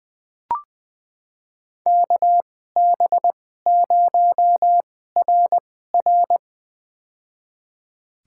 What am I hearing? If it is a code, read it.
Morse code, heard as a keyed tone, about 20 wpm, spaced as for 14 wpm, 700 Hz. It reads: KB0RR
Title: USA Call Signs - 5 Characters - 2x Character Spacing - 20wpm